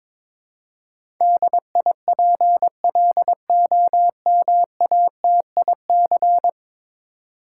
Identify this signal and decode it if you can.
Morse code, heard as a keyed tone, about 22 wpm, 700 Hz. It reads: DIPLOMATIC
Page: morse-code-ninja-practice-22wpm